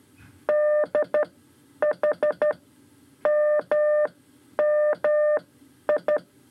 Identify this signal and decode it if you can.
Morse code, heard as a keyed tone, about 12 wpm, spaced as for 9 wpm, 600 Hz. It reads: DHMMI
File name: 5 Lezione CW 2